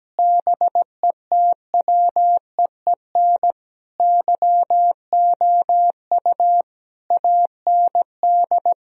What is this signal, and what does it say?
Morse code, heard as a keyed tone, about 17 wpm, 700 Hz. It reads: BETWEEN YOU AND
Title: Sentences from Top 1000 Words - Review Part 3 - 17wpm